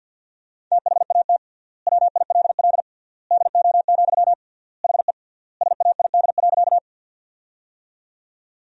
Morse code, heard as a keyed tone, about 50 wpm, 700 Hz. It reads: THAT WILL DO, HE SAID.
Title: 3x Word-Spacing - Sentences from Top 100 Words - 50wpm